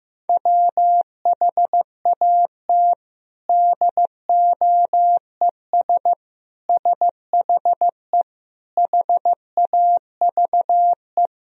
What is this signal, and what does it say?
Morse code, heard as a keyed tone, about 15 wpm, 700 Hz. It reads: WHAT DOES SHE HAVE
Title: Sentences from Top 1000 Words - Review Part 3 - 15wpm